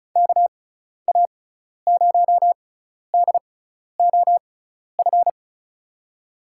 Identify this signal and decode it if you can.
Morse code, heard as a keyed tone, about 36 wpm, spaced as for 12 wpm, 690 Hz. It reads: KA0DOF
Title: USA Call Signs - 6 Characters - 6x Character Spacing - 35wpm